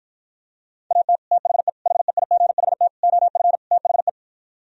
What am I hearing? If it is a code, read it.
Morse code, heard as a keyed tone, about 52 wpm, 700 Hz. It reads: AT THE HEIGHT OF THE